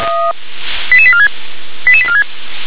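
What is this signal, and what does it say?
Five-tone selective call (ZVEI), heard as quick runs of five tones, tones of about 70 ms. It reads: 88846 88846
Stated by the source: ZVEI five-tone selective call received on SDR.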